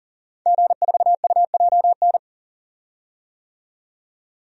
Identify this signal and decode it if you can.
Morse code, heard as a keyed tone, about 40 wpm, 700 Hz. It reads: G4UJN
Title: DXCC-100 Call Signs - 5 Characters - 40wpm